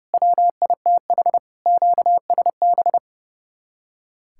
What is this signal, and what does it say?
Morse code, heard as a keyed tone, about 37 wpm, 700 Hz. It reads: WITH QSB
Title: Speed-Racing Common QSO Phrases - 30wpm